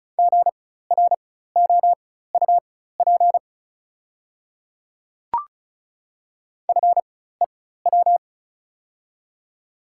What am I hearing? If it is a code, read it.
Morse code, heard as a keyed tone, about 36 wpm, spaced as for 16 wpm, 700 Hz. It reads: GROUP FEW
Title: 4x Character Spacing - Top 500 Words - 35wpm